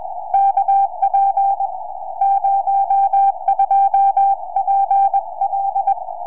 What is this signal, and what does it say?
Morse code, heard as a keyed tone, about 21 wpm, spaced as for 17 wpm, 790 Hz. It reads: KP02PL
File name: oh6pa 4m